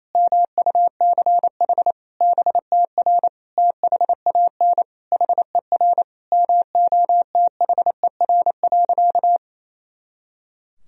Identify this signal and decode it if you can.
Morse code, heard as a keyed tone, about 28 wpm, 700 Hz. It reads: MUCH BTR THAN HER MOTHER.
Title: Sentences from Top 400 Words and Abbreviations 28wpm